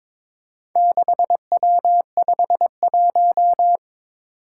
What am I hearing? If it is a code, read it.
Morse code, heard as a keyed tone, about 22 wpm, 700 Hz. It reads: 6W51
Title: Four Challenging Letter-Number Combinations 22wpm